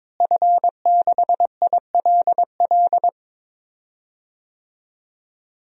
Morse code, heard as a keyed tone, about 22 wpm, 700 Hz. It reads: F6ILL
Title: DXCC-100 Call Signs - 5 Characters - Repeated 2x - 22wpm